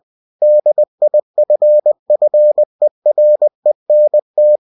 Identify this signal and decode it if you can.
Morse code, heard as a keyed tone, about 20 wpm, 600 Hz. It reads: DIFFERENT
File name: cw-books / winnie-the-pooh / winnie-20wpm-page-0095